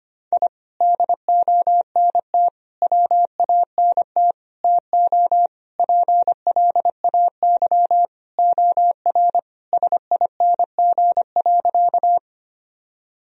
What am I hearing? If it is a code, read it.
Morse code, heard as a keyed tone, about 25 wpm, 700 Hz. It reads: I DONT WANT TO PLAY OR SING.
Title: Sentences from Top 1000 Words - Review Part 2 - 25wpm